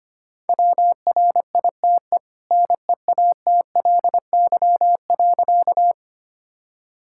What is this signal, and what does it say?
Morse code, heard as a keyed tone, about 25 wpm, 700 Hz. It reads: WRITE NEATLY.
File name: Easy Sentences with Hints - 25wpm